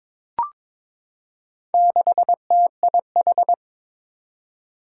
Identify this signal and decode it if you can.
Morse code, heard as a keyed tone, about 22 wpm, 700 Hz. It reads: 6TIH
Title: Four Challenging Letter-Number Combinations 22wpm